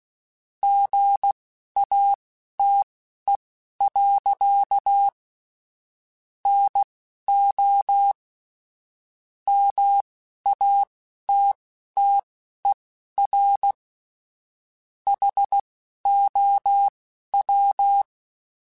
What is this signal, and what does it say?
Morse code, heard as a keyed tone, about 16 wpm, spaced as for 11 wpm, 800 Hz. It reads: GATE. NO MATTER HOW
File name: Technical Subjects / Technical Subjects-mp3s / AND Gate @10